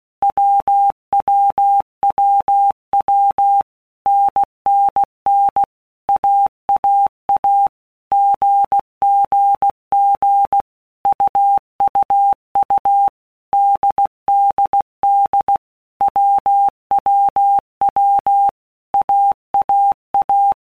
Morse code, heard as a keyed tone, about 16 wpm, 800 Hz. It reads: WWWW NNN AAA GGG UUU DDD WWW AAA
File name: All Groups 1-minute Blocks / Groups@15 / AWUNDG